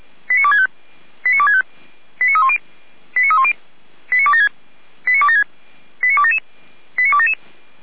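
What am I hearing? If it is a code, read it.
Five-tone selective call (ZVEI), heard as quick runs of five tones, tones of about 80 ms. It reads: 89265 89265 89310 89310 89276 89276 89277 89277
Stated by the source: ZVEI five-tone selective call received by SDR, fire brigade Feuerwehr Orlamünde.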